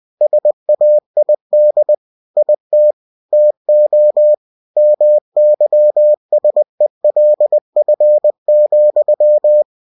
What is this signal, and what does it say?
Morse code, heard as a keyed tone, about 20 wpm, 600 Hz. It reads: SAID IT TO MYSELF,
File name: cw-books / winnie-the-pooh / winnie-20wpm-page-0005